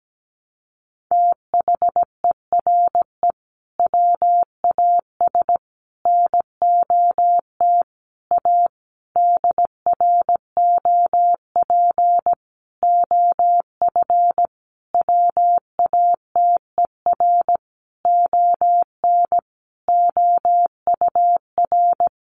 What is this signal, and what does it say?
Morse code, heard as a keyed tone, about 17 wpm, 700 Hz. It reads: THERE WAS NOT A DROP OF WATER ON OUR
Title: Sentences from Top 1400 Words 17wpm